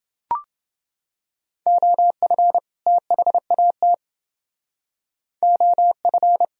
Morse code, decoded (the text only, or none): OF THAT OF